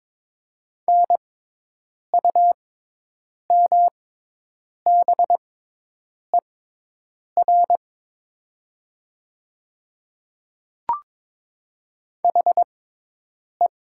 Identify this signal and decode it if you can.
Morse code, heard as a keyed tone, about 22 wpm, spaced as for 8 wpm, 700 Hz. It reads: NUMBER HE
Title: 6x Character Spacing - Top 100 Words - 22wpm